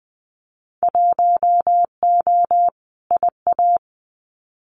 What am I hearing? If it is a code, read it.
Morse code, heard as a keyed tone, about 20 wpm, 700 Hz. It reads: 1O IA
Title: Winter Field Day Exchanges Encore 20wpm